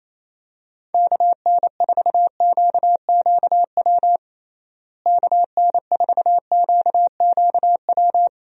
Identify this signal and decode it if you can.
Morse code, heard as a keyed tone, about 28 wpm, 700 Hz. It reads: KN4QQW KN4QQW
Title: USA Call Signs - 6 Characters - Repeated 2x - 28wpm